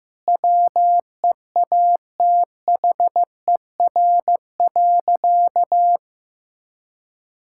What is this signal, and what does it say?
Morse code, heard as a keyed tone, about 14 wpm, 700 Hz. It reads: WEATHER.
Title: Sentences from Top 1000 Words - Review Part 5 - 15wpm